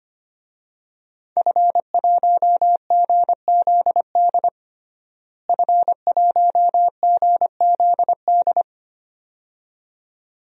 Morse code, heard as a keyed tone, about 25 wpm, 700 Hz. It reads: F1GZD F1GZD
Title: DXCC-100 Call Signs - 5 Characters - Repeated 2x - 25wpm